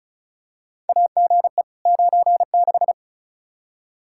Morse code, heard as a keyed tone, about 36 wpm, 700 Hz. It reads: AGE 96
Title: Practice QSO - Exchange 3 - More Info - 35wpm